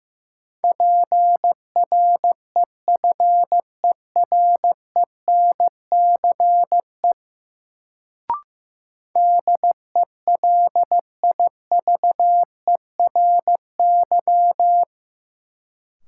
Morse code, heard as a keyed tone, about 15 wpm, 700 Hz. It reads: PREFERENCE DELIVERY
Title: Sentences from Top 1500 Words - New Word Drill Alt - 15wpm